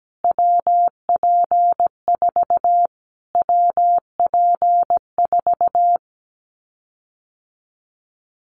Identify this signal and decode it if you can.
Morse code, heard as a keyed tone, about 17 wpm, 700 Hz. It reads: WP4 WP4
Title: Callsign Prefixes for Top 25 DXCC - Repeated 3x - 17wpm